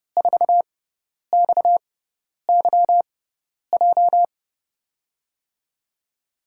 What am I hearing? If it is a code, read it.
Morse code, heard as a keyed tone, about 29 wpm, spaced as for 10 wpm, 700 Hz. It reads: 4XYJ
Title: USA Call Signs - 5 Characters - 6x Character Spacing - 30wpm